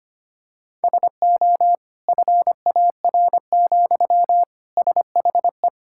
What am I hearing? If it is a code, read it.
Morse code, heard as a keyed tone, about 25 wpm, 700 Hz. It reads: SO FAR, SHE
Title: Sentences from Top 1000 Words - Review Part 6 - 25wpm